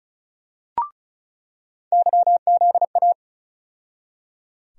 Morse code, heard as a keyed tone, about 34 wpm, 700 Hz. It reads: YZA
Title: USA Call Sign Postfixes - 3 Letters 35wpm